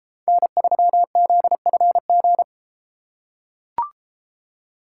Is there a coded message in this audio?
Yes: Morse code from 0.3 to 2.4 s, heard as a keyed tone, about 30 wpm, 700 Hz. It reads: N3ZFG